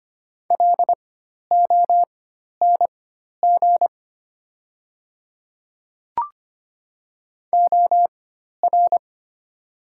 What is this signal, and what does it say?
Morse code, heard as a keyed tone, about 25 wpm, spaced as for 12 wpm, 700 Hz. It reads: LONG OR
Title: 4x Character Spacing - Top 100 Words - 25wpm